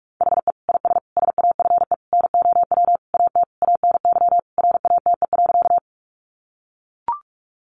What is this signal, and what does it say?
Morse code, heard as a keyed tone, about 44 wpm, 700 Hz. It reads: HE IS SAFE NOW AT ANY RATE.
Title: Sentences from Top 900 Words 45wpm